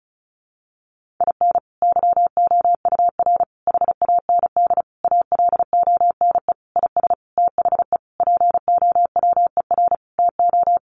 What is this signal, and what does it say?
Morse code, heard as a keyed tone, about 34 wpm, 700 Hz. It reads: IN YOUR HAND ALONE IS THE POWER TO
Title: Sentences from Top 1400 Words 35wpm